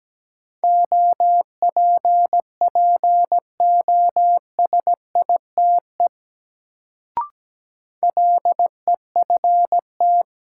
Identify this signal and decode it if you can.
Morse code, heard as a keyed tone, about 17 wpm, 700 Hz. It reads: OPPOSITE LEFT
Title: Sets of 5 Words Encore - New Word Drill - 17wpm